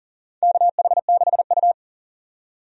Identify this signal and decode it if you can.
Morse code, heard as a keyed tone, about 37 wpm, 700 Hz. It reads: KH6U
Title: DXCC-100 Call Signs - 4 Characters - 40wpm